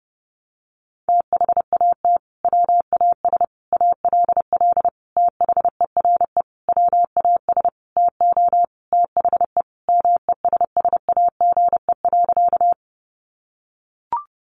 Morse code, heard as a keyed tone, about 30 wpm, 700 Hz. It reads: THAT WAS ALL THERE WAS TO THE MESSAGE.
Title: Sentences from Top 1100 Words 30wpm